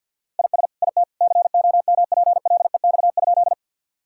Morse code, heard as a keyed tone, about 50 wpm, 700 Hz. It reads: IS IT COMPLEX?
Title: Sentences from Top 1000 Words - Review Part 6 - 50wpm